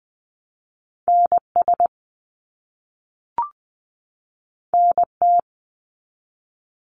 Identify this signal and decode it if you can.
Morse code, heard as a keyed tone, about 20 wpm, 700 Hz. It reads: NS NT
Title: Canadian Provinces and Territories 20wpm